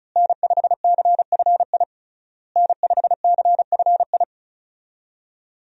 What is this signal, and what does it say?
Morse code, heard as a keyed tone, about 36 wpm, 700 Hz. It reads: N5CFI N5CFI
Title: USA Call Signs - 5 Characters - Repeated 3x - 35wpm